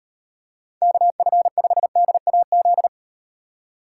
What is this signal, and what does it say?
Morse code, heard as a keyed tone, about 35 wpm, 700 Hz. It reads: KF5DAZ